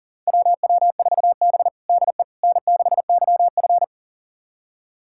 Morse code, heard as a keyed tone, about 40 wpm, 700 Hz. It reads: WW4B DE N6YF
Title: USA Callsign DE Callsign - 4 Characters - 40wpm